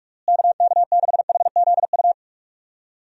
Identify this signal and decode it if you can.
Morse code, heard as a keyed tone, about 41 wpm, 700 Hz. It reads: KK6HZU
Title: USA Call Signs - 6 Characters - Repeated 3x - 45wpm